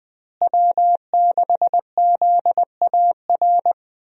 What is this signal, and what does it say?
Morse code, heard as a keyed tone, about 20 wpm, 700 Hz. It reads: W6ZAR
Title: USA Call Signs - 5 Characters - Repeated 2x - 20wpm